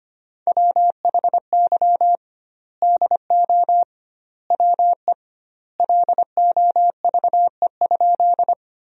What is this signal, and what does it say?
Morse code, heard as a keyed tone, about 25 wpm, 700 Hz. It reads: WHY DO WE LOVE?
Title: 2x Word-Spacing - Sentences from Top 400 Words - 25wpm